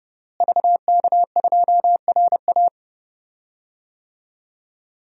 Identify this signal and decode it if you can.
Morse code, heard as a keyed tone, about 32 wpm, 700 Hz. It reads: VK2RA
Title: DXCC-100 Call Signs - 5 Characters - Repeated 3x - 30wpm